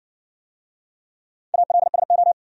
Morse code, heard as a keyed tone, about 60 wpm, 700 Hz. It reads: ALSO